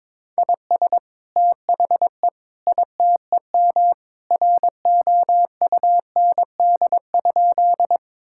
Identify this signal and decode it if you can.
Morse code, heard as a keyed tone, about 22 wpm, 700 Hz. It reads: IS THE ITEM ROUND?
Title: Sentences from Top 1000 Words - Review Part 6 - 22wpm